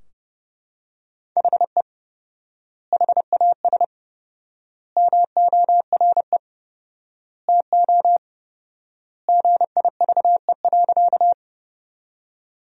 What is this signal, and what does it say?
Morse code, heard as a keyed tone, about 29 wpm, 700 Hz. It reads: HE HAS MORE TO GIVE.
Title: 4x Word-Spacing - Sentences from Top 200 Words - 30wpm